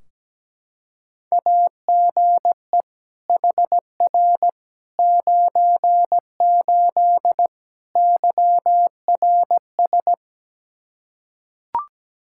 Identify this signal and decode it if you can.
Morse code, heard as a keyed tone, about 17 wpm, 700 Hz. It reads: AGE HR 98 YRS
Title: K1USN SST QSO Practice 17wpm